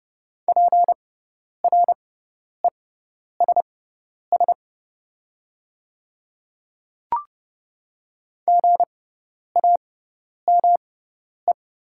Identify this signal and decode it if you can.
Morse code, heard as a keyed tone, about 30 wpm, spaced as for 10 wpm, 690 Hz. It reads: PRESS GAME
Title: 6x Character Spacing - Top 500 Words - 30wpm